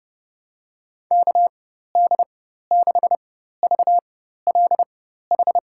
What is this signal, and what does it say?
Morse code, heard as a keyed tone, about 31 wpm, spaced as for 14 wpm, 700 Hz. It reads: KD6VLH
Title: USA Call Signs - 6 Characters - 4x Character Spacing - 30wpm